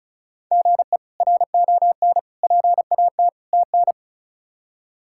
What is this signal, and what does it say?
Morse code, heard as a keyed tone, about 36 wpm, 700 Hz. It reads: GE RON PAT TN
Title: K1USN SST QSO Practice - 35wpm